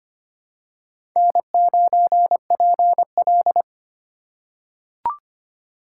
Morse code, heard as a keyed tone, about 25 wpm, 700 Hz. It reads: N9PL
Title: USA Call Signs - 4 Characters - 4x Character Spacing - 25wpm